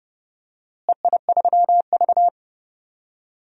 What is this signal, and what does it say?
Morse code, heard as a keyed tone, about 31 wpm, 700 Hz. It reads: EI3V
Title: DXCC-100 Call Signs - 4 Characters - Repeated 3x - 30wpm